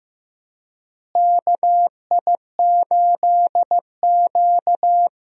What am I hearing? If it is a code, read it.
Morse code, heard as a keyed tone, about 15 wpm, 700 Hz. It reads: KI8Q
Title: USA Call Signs - 4 Characters - Repeated 2x - 15wpm